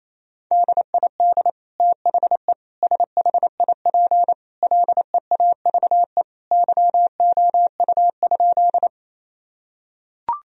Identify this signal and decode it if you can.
Morse code, heard as a keyed tone, about 28 wpm, 700 Hz. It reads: DID THE SHIP LEAVE YOU?
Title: Sentences from Top 1000 Words - Review Part 2 - 28wpm